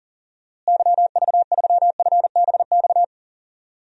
Morse code, heard as a keyed tone, about 40 wpm, 700 Hz. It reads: YV3FBX
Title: DXCC-100 Call Signs - 6 Characters - Repeated 3x - 40wpm